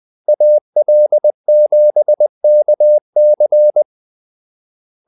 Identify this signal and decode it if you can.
Morse code, heard as a keyed tone, about 20 wpm, 600 Hz. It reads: AL7KC
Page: Kaldesignaler 20 wpm